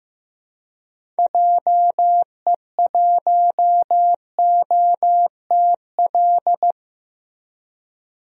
Morse code, heard as a keyed tone, about 15 wpm, 700 Hz. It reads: JE1OTL
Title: DXCC-100 Call Signs - 6 Characters - Repeated 2x - 15wpm